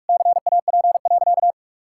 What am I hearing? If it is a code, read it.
Morse code, heard as a keyed tone, about 46 wpm, 700 Hz. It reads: CAP.
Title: Sentences from Top 1600 Words 45wpm